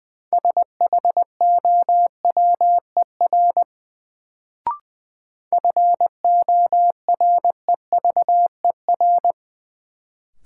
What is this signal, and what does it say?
Morse code, heard as a keyed tone, about 20 wpm, 700 Hz. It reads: SHOWER FOREVER
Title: Sentences from Top 1600 Words - New Word Drill - 20wpm